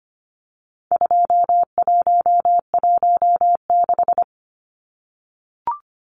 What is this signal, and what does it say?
Morse code, heard as a keyed tone, about 25 wpm, 700 Hz. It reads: 2116
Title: Numbers - 4-Digit 25wpm